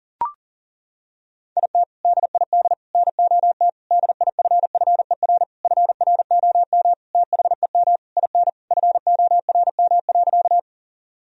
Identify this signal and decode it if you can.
Morse code, heard as a keyed tone, about 40 wpm, 700 Hz. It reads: IT DID NOT DIFFER FROM THEM IN FORM.
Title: Sentences from Top 1000 Words - Review Part 1 - 40wpm